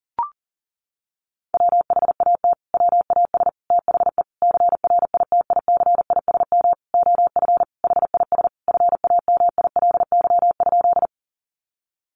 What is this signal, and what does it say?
Morse code, heard as a keyed tone, about 40 wpm, 700 Hz. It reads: WHAT WAS THE CRITICISM OF HIS FAMILY?
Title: Sentences from Top 1300 Words 40wpm